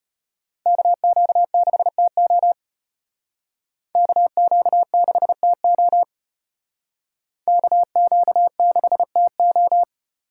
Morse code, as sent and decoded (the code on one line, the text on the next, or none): -.- --.- -.... - --- / -.- --.- -.... - --- / -.- --.- -.... - ---
KQ6TO KQ6TO KQ6TO